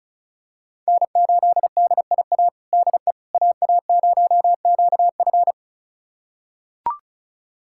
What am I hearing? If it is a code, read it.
Morse code, heard as a keyed tone, about 34 wpm, 700 Hz. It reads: N8DIA DE AA0QF
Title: USA Callsign DE Callsign - 5 Characters - 35wpm